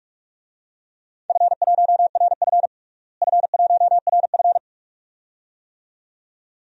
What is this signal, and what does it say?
Morse code, heard as a keyed tone, about 41 wpm, 700 Hz. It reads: F1RF F1RF
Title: DXCC-100 Call Signs - 4 Characters - Repeated 2x - 45wpm